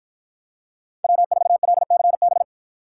Morse code, heard as a keyed tone, about 53 wpm, 700 Hz. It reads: W4LCB